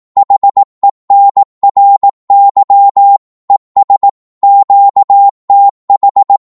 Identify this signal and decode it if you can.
Morse code, heard as a keyed tone, about 18 wpm, 800 Hz. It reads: HENRY ES QTH
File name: Morse / qso-18wpm